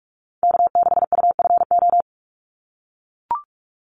Morse code, heard as a keyed tone, about 44 wpm, 700 Hz. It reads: K6UFO
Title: Call Signs - 45wpm